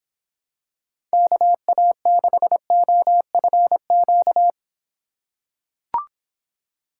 Morse code, heard as a keyed tone, about 25 wpm, 700 Hz. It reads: KA6OFQ